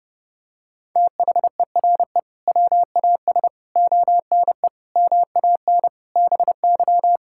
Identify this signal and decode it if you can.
Morse code, heard as a keyed tone, about 32 wpm, 700 Hz. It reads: THERE WAS ONE MAN BY
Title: Sentences from Top 1600 Words 30wpm